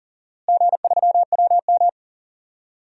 Morse code, heard as a keyed tone, about 40 wpm, 700 Hz. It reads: G3WM
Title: DXCC-100 Call Signs - 4 Characters - Repeated 2x - 40wpm